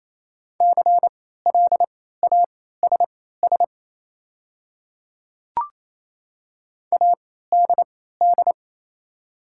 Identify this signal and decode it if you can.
Morse code, heard as a keyed tone, about 27 wpm, spaced as for 16 wpm, 690 Hz. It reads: CLASS ADD
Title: 3x Character Spacing - Top 500 Words - 28wpm